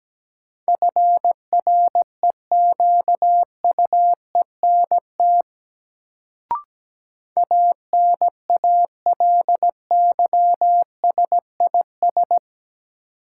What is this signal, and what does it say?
Morse code, heard as a keyed tone, about 17 wpm, 700 Hz. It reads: FREQUENT ANALYSIS
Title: Sentences from Top 1400 Words - New Word Drill Alt - 17wpm